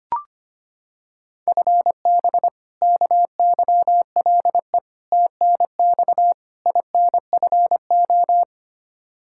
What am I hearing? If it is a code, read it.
Morse code, heard as a keyed tone, about 25 wpm, 700 Hz. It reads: FB KYLE TNX INFO
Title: Practice QSO - Exchange 3 - More Info - 25wpm